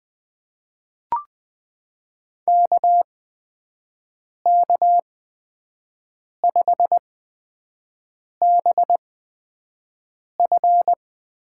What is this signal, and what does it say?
Morse code, heard as a keyed tone, about 20 wpm, spaced as for 5 wpm, 700 Hz. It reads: KK5BF
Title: USA Call Signs - 5 Characters - 8x Character Spacing - 20wpm